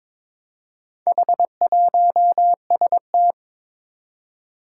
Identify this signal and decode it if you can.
Morse code, heard as a keyed tone, about 22 wpm, 700 Hz. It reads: H1ST
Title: Four Challenging Letter-Number Combinations 22wpm